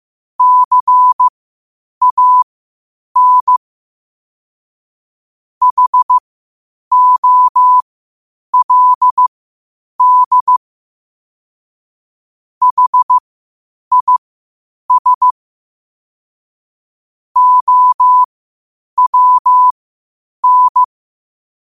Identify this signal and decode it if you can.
Morse code, heard as a keyed tone, about 15 wpm, spaced as for 8 wpm, 1000 Hz. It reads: CAN HOLD HIS OWN
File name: qotdfiles / 8wpm